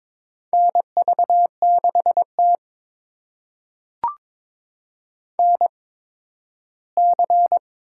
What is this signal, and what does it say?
Morse code, heard as a keyed tone, about 22 wpm, spaced as for 13 wpm, 700 Hz. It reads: NV6T NC
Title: USA Call Signs - 4 Characters - 8x Character Spacing - 22wpm